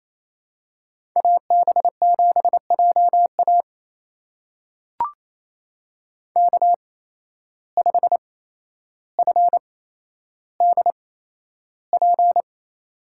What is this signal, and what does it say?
Morse code, heard as a keyed tone, about 27 wpm, spaced as for 12 wpm, 700 Hz. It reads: AB7JA K5FDP
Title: USA Call Signs - 5 Characters - 8x Character Spacing - 28wpm